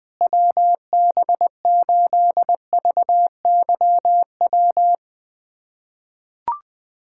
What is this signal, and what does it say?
Morse code, heard as a keyed tone, about 21 wpm, 700 Hz. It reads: WB8VYW
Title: USA Callsign DE Callsign 20wpm